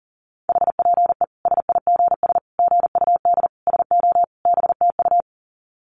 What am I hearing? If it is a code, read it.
Morse code, heard as a keyed tone, about 38 wpm, 700 Hz. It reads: HPE SIGS GUD SO BTU
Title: Common QSO Phrases 40wpm